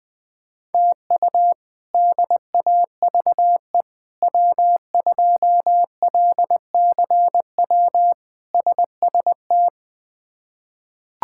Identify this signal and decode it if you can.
Morse code, heard as a keyed tone, about 20 wpm, 700 Hz. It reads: TU DAVE W2LCW SST
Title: K1USN SST QSO Practice - 20wpm